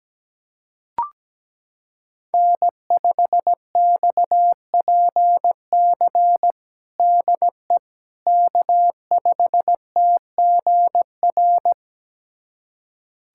Morse code, heard as a keyed tone, about 17 wpm, 700 Hz. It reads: N5XPC DE K5TGR
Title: USA Callsign DE Callsign - 5 Characters - 17wpm